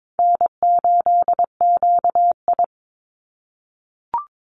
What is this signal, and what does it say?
Morse code, heard as a keyed tone, about 22 wpm, 700 Hz. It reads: N8QI
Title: USA Call Signs 22wpm